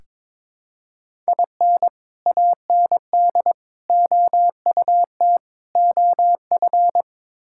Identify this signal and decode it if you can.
Morse code, heard as a keyed tone, about 22 wpm, 700 Hz. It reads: IN AND OUT OF
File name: practice / 22wpm / Build A Phrase - 5-Word Phrases - 22wpm